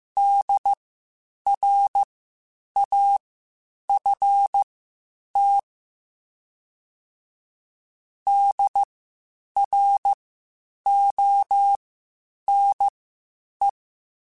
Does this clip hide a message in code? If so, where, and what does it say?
Yes: Morse code from 0.2 to 13.7 s, heard as a keyed tone, about 15 wpm, spaced as for 8 wpm, 800 Hz. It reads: DRAFT DRONE